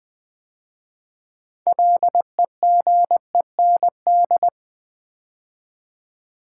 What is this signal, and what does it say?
Morse code, heard as a keyed tone, about 20 wpm, 700 Hz. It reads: LEGEND
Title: 6 Letter Words - First 3 Letters Given - 20wpm